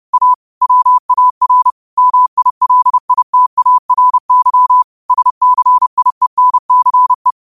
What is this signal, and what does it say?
Morse code, heard as a keyed tone, about 30 wpm, 1000 Hz. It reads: A WAR MILITARY SCIENCE
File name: qotdfiles / 30wpm